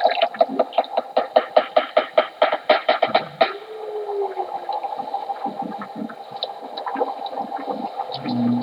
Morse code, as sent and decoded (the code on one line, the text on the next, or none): none